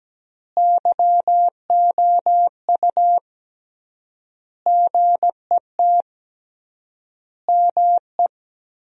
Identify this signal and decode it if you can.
Morse code, heard as a keyed tone, about 17 wpm, 700 Hz. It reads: YOU GET ME
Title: 3x Word-Spacing - Sentences from Top 200 Words - 17wpm